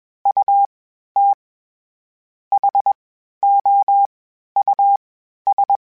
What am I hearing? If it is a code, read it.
Morse code, heard as a keyed tone, about 21 wpm, spaced as for 10 wpm, 800 Hz. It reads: UTHOUS
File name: General / General-mp3s / Houses @20